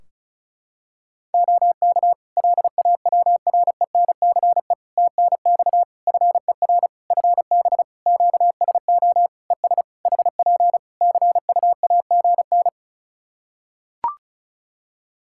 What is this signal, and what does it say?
Morse code, heard as a keyed tone, about 36 wpm, 700 Hz. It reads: OK LAWRENCE TNX FER FB QSO ES HP CUAGN
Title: Practice QSO - Exchange 4 - Ending - 35wpm